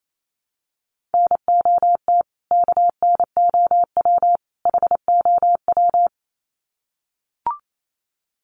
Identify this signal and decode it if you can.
Morse code, heard as a keyed tone, about 28 wpm, 700 Hz. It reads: NOT KNOW HOW
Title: Sets of 3 Words 28wpm